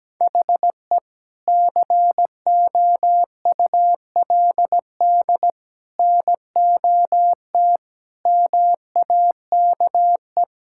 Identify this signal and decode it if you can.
Morse code, heard as a keyed tone, about 17 wpm, 700 Hz. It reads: HE COULD NOT MAKE
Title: Sentences from Top 1600 Words 17wpm